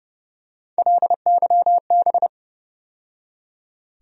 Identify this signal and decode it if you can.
Morse code, heard as a keyed tone, about 30 wpm, 700 Hz. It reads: LYB
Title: USA Call Sign Postfixes - 3 Letters 30wpm